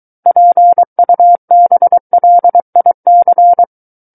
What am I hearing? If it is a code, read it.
Morse code, heard as a keyed tone, about 23 wpm, 700 Hz. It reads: PUBLIC
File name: Book One / 14-Chapter 14 - 23 WPM